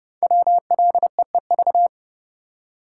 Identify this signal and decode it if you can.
Morse code, heard as a keyed tone, about 30 wpm, 700 Hz. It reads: WLEEV
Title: Code Groups - 5 Letters - 30wpm